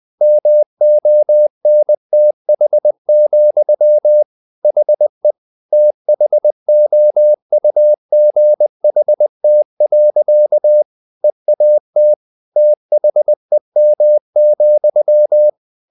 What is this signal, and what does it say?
Morse code, heard as a keyed tone, about 20 wpm, 600 Hz. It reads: MONTH, HE THOUGHT. EAT THEM,
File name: the old man and the sea / the old man and the sea part 1 20wpm